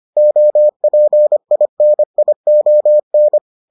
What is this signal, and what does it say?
Morse code, heard as a keyed tone, about 25 wpm, 600 Hz. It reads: OPINION